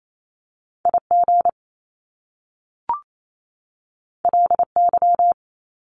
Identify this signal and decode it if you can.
Morse code, heard as a keyed tone, about 28 wpm, 700 Hz. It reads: IG LY
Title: Top Two Letter Combinations in Words 28wpm